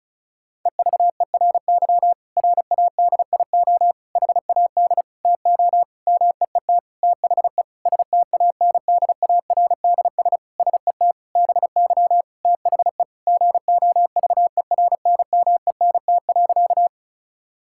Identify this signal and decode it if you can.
Morse code, heard as a keyed tone, about 34 wpm, 700 Hz. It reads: EVERY RADIO HAD TO MEET THE STANDARDS SET BY THE GOVERNMENT.
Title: Sentences from Top 1000 Words - Review Part 6 - 35wpm